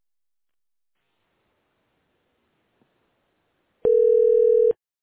Ringback tone (North America), 440 + 480 Hz.